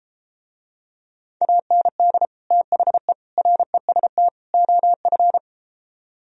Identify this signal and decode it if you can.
Morse code, heard as a keyed tone, about 33 wpm, 700 Hz. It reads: AND THE REST OF